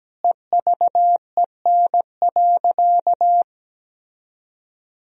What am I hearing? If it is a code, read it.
Morse code, heard as a keyed tone, about 17 wpm, 700 Hz. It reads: EVEN.
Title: Sentences from Top 1000 Words - Review Part 1 - 17wpm